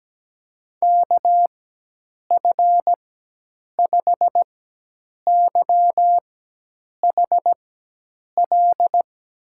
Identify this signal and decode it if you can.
Morse code, heard as a keyed tone, about 17 wpm, spaced as for 8 wpm, 700 Hz. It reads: KF5YHL
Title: USA Call Signs - 6 Characters - 4x Character Spacing - 17wpm